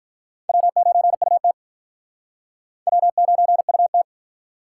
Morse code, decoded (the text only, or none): W9UT W9UT